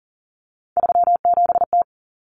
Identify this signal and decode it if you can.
Morse code, heard as a keyed tone, about 40 wpm, 700 Hz. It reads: 37T
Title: Common Cut Numbers - 3-Digit 40wpm